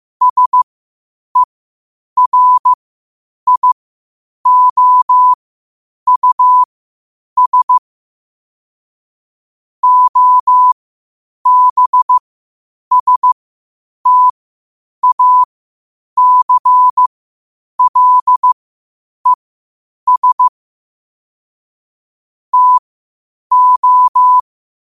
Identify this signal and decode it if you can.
Morse code, heard as a keyed tone, about 15 wpm, spaced as for 8 wpm, 1000 Hz. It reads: SERIOUS OBSTACLES TO
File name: qotdfiles / 8wpm